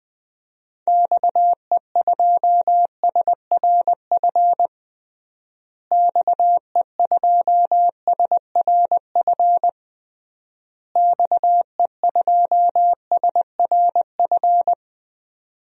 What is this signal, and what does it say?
Morse code, heard as a keyed tone, about 21 wpm, 700 Hz. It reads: XE2SRF XE2SRF XE2SRF
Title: DXCC-100 Call Signs - 6 Characters - Repeated 3x - 20wpm